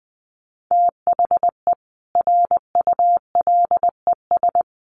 Morse code, heard as a keyed tone, about 20 wpm, 700 Hz. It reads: THE RULES
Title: Sets of 5 Words Encore2 20wpm